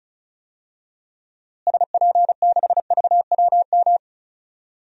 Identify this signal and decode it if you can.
Morse code, heard as a keyed tone, about 36 wpm, 700 Hz. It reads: SP6VWM
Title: DXCC-100 Call Signs - 6 Characters - Repeated 2x - 35wpm